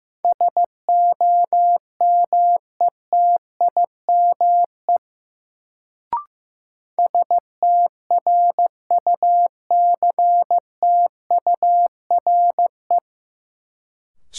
Morse code, heard as a keyed tone, about 15 wpm, 700 Hz. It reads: SOMETIME STRUCTURE
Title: Sentences from Top 1000 Words - Review Part 6 - New Word Drill - 15wpm